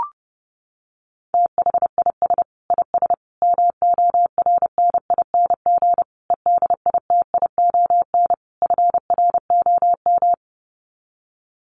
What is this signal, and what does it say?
Morse code, heard as a keyed tone, about 30 wpm, 700 Hz. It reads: THIS IS MORNING EDITION FROM
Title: Sets of 5 Words Encore2 30wpm